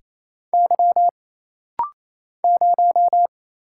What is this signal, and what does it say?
Morse code, heard as a keyed tone, about 27 wpm, 700 Hz. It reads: Y 0
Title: Single Letter-Number - Rapid-Fire 28wpm